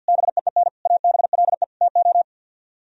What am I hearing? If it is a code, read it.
Morse code, heard as a keyed tone, about 63 wpm, 700 Hz. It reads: BEEN ABLE TO